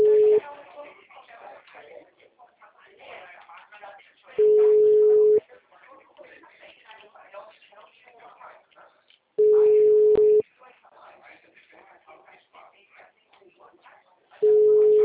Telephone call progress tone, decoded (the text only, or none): ringback tone